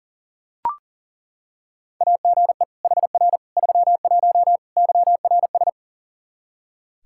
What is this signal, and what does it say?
Morse code, heard as a keyed tone, about 40 wpm, 700 Hz. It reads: AGE HR 31 YRS
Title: Practice QSO - Exchange 3 - More Info - 40wpm